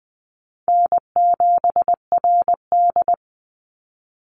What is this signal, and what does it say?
Morse code, heard as a keyed tone, about 20 wpm, 700 Hz. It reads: N7RD
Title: Call Signs Repeated 3x - 20wpm